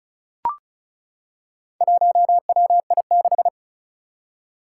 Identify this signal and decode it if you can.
Morse code, heard as a keyed tone, about 36 wpm, 700 Hz. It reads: 1WI6
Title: Four Challenging Letter-Number Combinations 35wpm